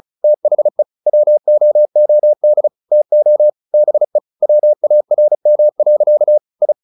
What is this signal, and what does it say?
Morse code, heard as a keyed tone, about 34 wpm, 600 Hz. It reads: THE WOOD TO BE WARM. I
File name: the old man and the sea / the old man and the sea 1 35wpm